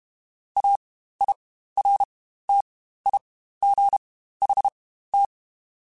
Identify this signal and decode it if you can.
Morse code, heard as a keyed tone, about 32 wpm, spaced as for 15 wpm, 800 Hz. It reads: AIRTIGHT
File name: Quotes Spaced / Quotes Spaced-mp3s / Airtight sp @30